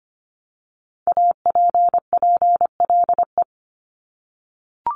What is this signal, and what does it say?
Morse code, heard as a keyed tone, about 25 wpm, 700 Hz. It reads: APPLE
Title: Multi-letter Phonemes Repeated 3x 25wpm